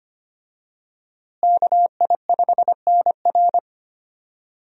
Morse code, heard as a keyed tone, about 25 wpm, 700 Hz. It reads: KI5NR
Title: USA Call Signs - 5 Characters - Repeated 2x - 25wpm